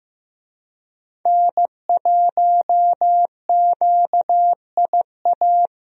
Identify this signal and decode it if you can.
Morse code, heard as a keyed tone, about 15 wpm, 700 Hz. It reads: N1QIA
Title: USA Call Signs - 5 Characters - Repeated 3x - 15wpm